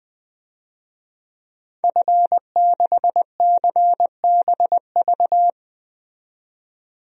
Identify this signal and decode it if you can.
Morse code, heard as a keyed tone, about 20 wpm, 700 Hz. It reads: F6CBV
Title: DXCC-100 Call Signs - 5 Characters - 20wpm